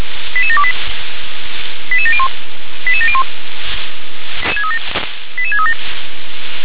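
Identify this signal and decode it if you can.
Five-tone selective call (ZVEI), heard as quick runs of five tones, tones of about 70 ms. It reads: 88728 88701 88701 88637
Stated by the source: Received on SDR as a ZVEI five-tone selective call.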